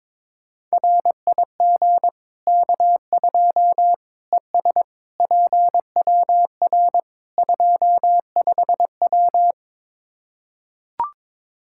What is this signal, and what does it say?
Morse code, heard as a keyed tone, about 22 wpm, 700 Hz. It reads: RIG K2 ES PWR 25W
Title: Practice QSO - Exchange 2 - Info - 22wpm